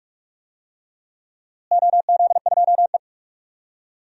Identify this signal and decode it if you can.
Morse code, heard as a keyed tone, about 45 wpm, 700 Hz. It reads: OZ2E
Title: DXCC-100 Call Signs - 4 Characters - Repeated 3x - 45wpm